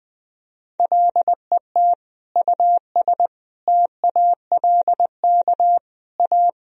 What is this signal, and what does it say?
Morse code, heard as a keyed tone, about 21 wpm, 700 Hz. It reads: LET US TALK A
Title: Sentences from Top 400 Words and Abbreviations 20wpm